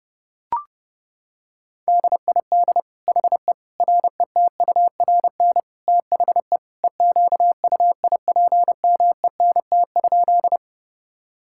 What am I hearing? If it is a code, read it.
Morse code, heard as a keyed tone, about 30 wpm, 700 Hz. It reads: DID HE RETURN THE EQUIPMENT?
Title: Sentences from Top 1000 Words - Review Part 5 - 30wpm